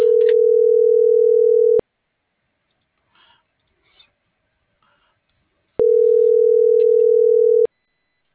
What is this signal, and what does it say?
Ringback tone (North America), 440 + 480 Hz.